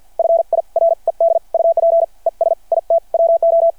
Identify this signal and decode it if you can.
Morse code, heard as a keyed tone, about 53 wpm, spaced as for 44 wpm, 660 Hz. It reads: FIRED UP ES IT WO
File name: archived / CRSnet